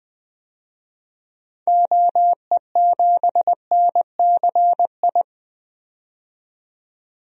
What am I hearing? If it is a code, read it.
Morse code, heard as a keyed tone, about 20 wpm, 700 Hz. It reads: OE7NCI
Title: DXCC-100 Call Signs - 6 Characters - 20wpm